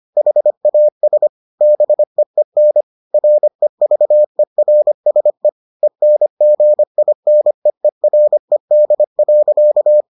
Morse code, heard as a keyed tone, about 25 wpm, 600 Hz. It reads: HAS BEEN REVERSE ENGINEERED.